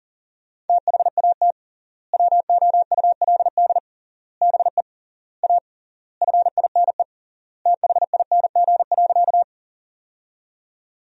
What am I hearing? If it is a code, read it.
Morse code, heard as a keyed tone, about 40 wpm, 700 Hz. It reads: THAT WOULD BE A FINE THING.
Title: 3x Word-Spacing - Sentences from Top 500 Words - 40wpm